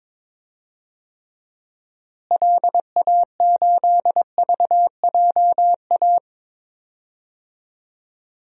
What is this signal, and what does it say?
Morse code, heard as a keyed tone, about 21 wpm, 700 Hz. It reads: LA8VJA
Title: DXCC-100 Call Signs - 6 Characters - Repeated 3x - 22wpm